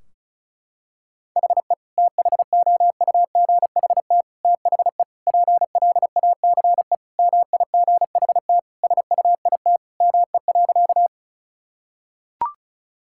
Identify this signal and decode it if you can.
Morse code, heard as a keyed tone, about 36 wpm, 700 Hz. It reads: HE THOUGHT THE PLACE MIGHT SUIT ME.
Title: Sentences from Top 1600 Words 35wpm